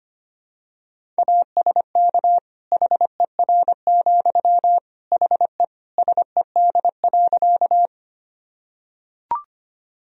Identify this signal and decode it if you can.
Morse code, heard as a keyed tone, about 25 wpm, 700 Hz. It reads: ASK HER, HE SED.
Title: Sentences from Top 200 Words and Abbreviations 25wpm